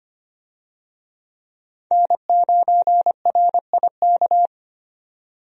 Morse code, heard as a keyed tone, about 25 wpm, 700 Hz. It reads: N9RIK
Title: USA Call Signs - 5 Characters - 25wpm